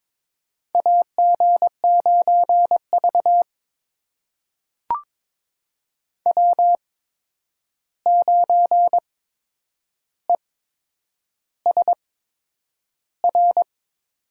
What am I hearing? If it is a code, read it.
Morse code, heard as a keyed tone, about 22 wpm, spaced as for 9 wpm, 700 Hz. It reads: AG9V W9ESR
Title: USA Call Signs - 8x Character Spacing - 22wpm